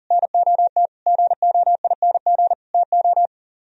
Morse code, heard as a keyed tone, about 51 wpm, 700 Hz. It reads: NOT GOING TO